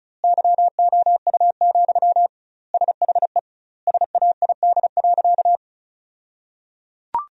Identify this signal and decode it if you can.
Morse code, heard as a keyed tone, about 36 wpm, 700 Hz. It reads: YOU, SHE SAID.
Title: 2x Word-Spacing - Sentences from Top 400 Words - 35wpm